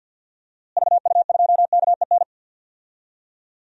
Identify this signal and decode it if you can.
Morse code, heard as a keyed tone, about 47 wpm, 700 Hz. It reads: VU2KEN
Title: DXCC-100 Call Signs - 6 Characters - Repeated 3x - 50wpm